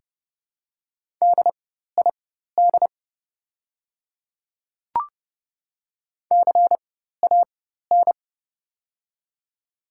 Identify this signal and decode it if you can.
Morse code, heard as a keyed tone, about 31 wpm, spaced as for 14 wpm, 700 Hz. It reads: DID CAN
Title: 4x Character Spacing - Top 300 Words - 30wpm